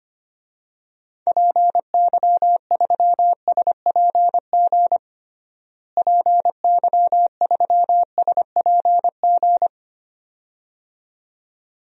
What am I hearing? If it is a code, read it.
Morse code, heard as a keyed tone, about 25 wpm, 700 Hz. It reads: PY3SPG PY3SPG
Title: DXCC-100 Call Signs - 6 Characters - Repeated 2x - 25wpm